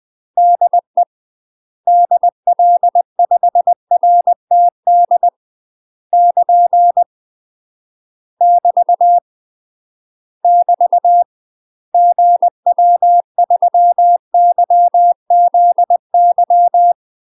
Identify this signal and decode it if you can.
Morse code, heard as a keyed tone, about 20 wpm, 700 Hz. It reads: DE DL5RTD <KN> <BT> <BT> GW3YZY
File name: UK MASTER QSOs TRAINING / 20WPM / UK MASTER QSOs 0008